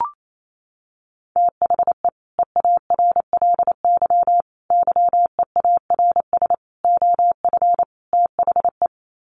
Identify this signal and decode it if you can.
Morse code, heard as a keyed tone, about 28 wpm, 700 Hz. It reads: THE EARLY YEARS OF THE
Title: Sets of 5 Words Encore 28wpm